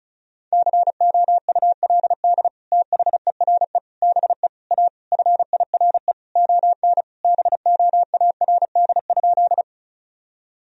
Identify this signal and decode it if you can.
Morse code, heard as a keyed tone, about 34 wpm, 700 Hz. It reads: COULD THERE BE A FIRE ON BOARD?
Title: Sentences from Top 1000 Words - Review Part 4 - 35wpm